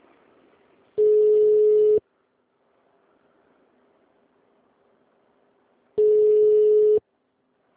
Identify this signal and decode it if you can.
Ringback tone (Europe, 425 Hz).